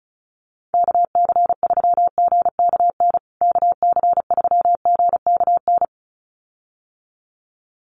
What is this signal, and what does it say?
Morse code, heard as a keyed tone, about 34 wpm, 700 Hz. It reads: KC3GKN KC3GKN
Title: USA Call Signs Repeated 2x 35wpm